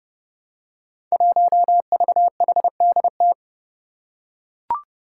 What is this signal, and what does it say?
Morse code, heard as a keyed tone, about 30 wpm, 700 Hz. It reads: 1VHDT
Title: Five Challenging Letter-Number Combinations 30wpm